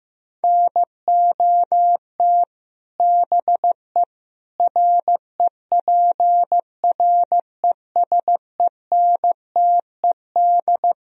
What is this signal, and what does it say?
Morse code, heard as a keyed tone, about 15 wpm, 700 Hz. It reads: NOT BE REPRESENTED
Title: Speed-Racing - Sets of 5 Words 15wpm